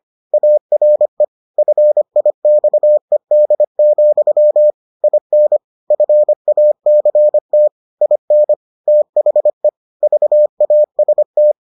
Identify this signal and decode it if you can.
Morse code, heard as a keyed tone, about 25 wpm, 600 Hz. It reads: ARE FIXED, IN FACT IN THE VAST